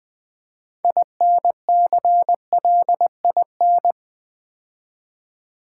Morse code, heard as a keyed tone, about 20 wpm, 700 Hz. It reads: INCLIN
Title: Build A Word - 2 Letters At A Time - 8-Letter Words - 20wpm